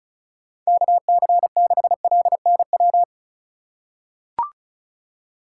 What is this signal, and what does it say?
Morse code, heard as a keyed tone, about 34 wpm, 700 Hz. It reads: KC6LNW
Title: USA Call Signs - 6 Characters - 2x Character Spacing - 35wpm